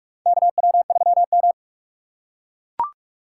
Morse code, heard as a keyed tone, about 42 wpm, 700 Hz. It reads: KW3M
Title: USA Call Signs - 4 Characters - 3x Character Spacing - 45wpm